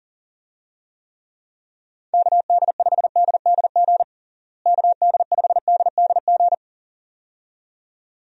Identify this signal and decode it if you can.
Morse code, heard as a keyed tone, about 40 wpm, 700 Hz. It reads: KD5DDG KD5DDG
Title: USA Call Signs - 6 Characters - Repeated 2x - 40wpm